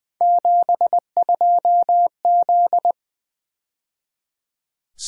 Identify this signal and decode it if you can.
Morse code, heard as a keyed tone, about 23 wpm, 700 Hz. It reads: 72Z
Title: Speed-Racing - Three Letter-Number Combinations - 20wpm